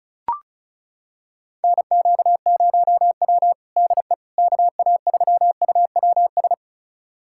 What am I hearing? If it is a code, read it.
Morse code, heard as a keyed tone, about 36 wpm, 700 Hz. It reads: NQ0W DE KA3UWS
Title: USA Callsign DE Callsign 35wpm